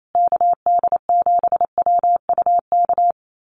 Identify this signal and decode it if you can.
Morse code, heard as a keyed tone, about 28 wpm, 700 Hz. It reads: KD7WUK
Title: USA Call Signs Repeated 3x 28wpm